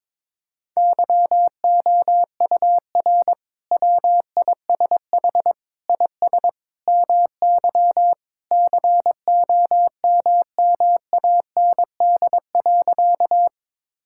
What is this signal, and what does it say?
Morse code, heard as a keyed tone, about 22 wpm, 700 Hz. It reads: YOUR WISH IS MY COMMAND.
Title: Sentences from Top 1600 Words 22wpm